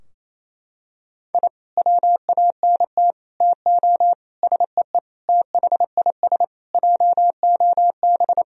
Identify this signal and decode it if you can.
Morse code, heard as a keyed tone, about 27 wpm, 700 Hz. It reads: I WANT TO SEE THIS JOB
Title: Sentences from Top 1000 Words - Review Part 3 - 28wpm